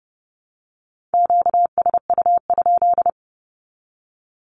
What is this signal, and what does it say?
Morse code, heard as a keyed tone, about 30 wpm, 700 Hz. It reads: QSU?
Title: Q-Codes 30wpm